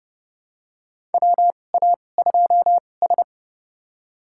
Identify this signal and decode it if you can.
Morse code, heard as a keyed tone, about 30 wpm, spaced as for 22 wpm, 700 Hz. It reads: WA2S
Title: USA Call Signs - 4 Characters - 2x Character Spacing - 30wpm